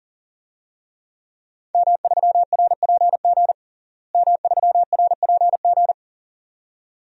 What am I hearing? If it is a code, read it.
Morse code, heard as a keyed tone, about 37 wpm, 700 Hz. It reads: M3RPG M3RPG
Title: DXCC-100 Call Signs - 5 Characters - Repeated 2x - 40wpm